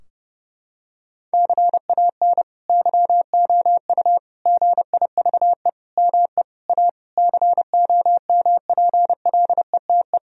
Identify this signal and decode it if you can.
Morse code, heard as a keyed tone, about 32 wpm, 700 Hz. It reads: CAN YOU GIVE ME A COMPLETE
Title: Sentences from Top 1000 Words - Review Part 2 - 30wpm